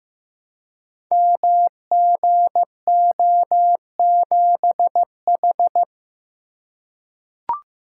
Morse code, heard as a keyed tone, about 15 wpm, 700 Hz. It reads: MGO7H
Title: Code Groups - 5 Letter:Numbers - 15wpm